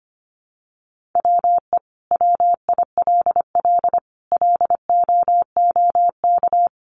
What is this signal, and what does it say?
Morse code, heard as a keyed tone, about 25 wpm, 700 Hz. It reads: WE WILL LOOK
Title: Sentences from Top 1400 Words 25wpm